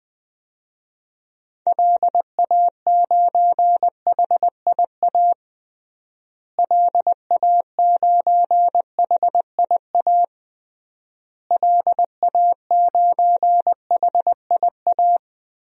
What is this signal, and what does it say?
Morse code, heard as a keyed tone, about 21 wpm, 700 Hz. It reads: LA9HIA LA9HIA LA9HIA
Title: DXCC-100 Call Signs - 6 Characters - Repeated 3x - 20wpm